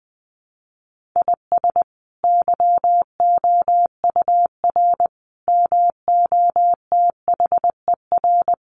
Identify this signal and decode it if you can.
Morse code, heard as a keyed tone, about 20 wpm, 700 Hz. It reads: IS YOUR MOTHER
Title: Sentences from Top 1500 Words 20wpm